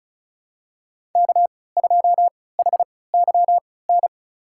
Morse code, heard as a keyed tone, about 34 wpm, spaced as for 20 wpm, 700 Hz. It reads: K2HYN
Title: USA Call Signs - 5 Characters - 3x Character Spacing - 35wpm